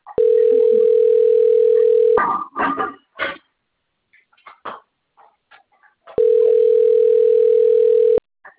Ringback tone (North America), 440 + 480 Hz.